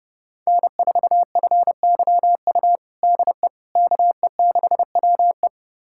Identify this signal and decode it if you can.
Morse code, heard as a keyed tone, about 29 wpm, 700 Hz. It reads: N4FYU DE KE6WE
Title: USA Callsign DE Callsign - 5 Characters - 30wpm